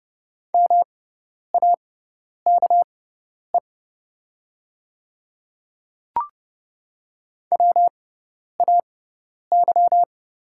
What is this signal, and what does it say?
Morse code, heard as a keyed tone, about 30 wpm, spaced as for 10 wpm, 690 Hz. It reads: MAKE WAY
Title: 6x Character Spacing - Top 100 Words - 30wpm